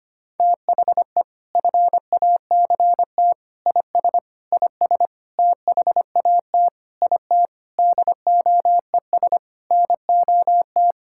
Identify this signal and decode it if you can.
Morse code, heard as a keyed tone, about 25 wpm, 700 Hz. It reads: THE FACT IS IS THAT IT DOES NOT
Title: Sentences from Top 1600 Words 25wpm